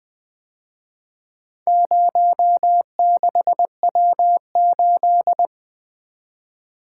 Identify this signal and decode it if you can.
Morse code, heard as a keyed tone, about 24 wpm, 700 Hz. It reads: 06W8
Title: Speed-Racing - Four Letter-Number Combinations - 20wpm